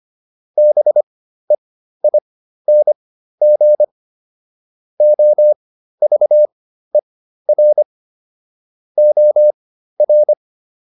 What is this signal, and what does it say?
Morse code, heard as a keyed tone, about 25 wpm, spaced as for 13 wpm, 600 Hz. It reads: BEING OVER OR